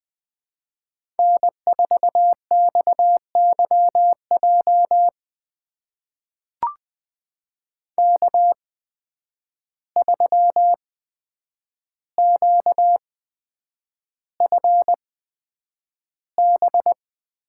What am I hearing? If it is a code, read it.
Morse code, heard as a keyed tone, about 20 wpm, spaced as for 9 wpm, 700 Hz. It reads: N4XYJ K3QFB
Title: USA Call Signs - 5 Characters - 8x Character Spacing - 20wpm